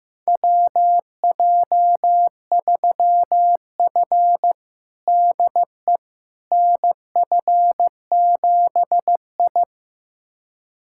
Morse code, heard as a keyed tone, about 15 wpm, 700 Hz. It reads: WJ3F DE NF7I
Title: USA Callsign DE Callsign - 4 Characters - 15wpm